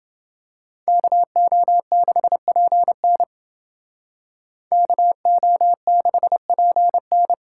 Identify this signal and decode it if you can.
Morse code, heard as a keyed tone, about 28 wpm, 700 Hz. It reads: KO6PN KO6PN